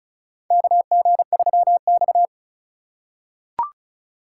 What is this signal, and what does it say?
Morse code, heard as a keyed tone, about 34 wpm, 700 Hz. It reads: KG3X
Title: USA Call Signs - 4 Characters - 3x Character Spacing - 35wpm